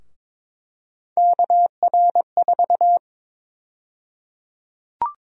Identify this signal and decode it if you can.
Morse code, heard as a keyed tone, about 22 wpm, 700 Hz. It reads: KR4
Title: USA Call Sign Prefixes - 22wpm